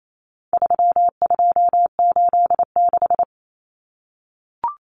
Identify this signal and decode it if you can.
Morse code, heard as a keyed tone, about 28 wpm, 700 Hz. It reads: 3286
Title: Common Cut Numbers - 4-Digit 28wpm